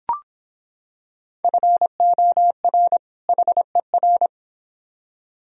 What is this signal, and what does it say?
Morse code, heard as a keyed tone, about 25 wpm, 700 Hz. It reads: FOR HER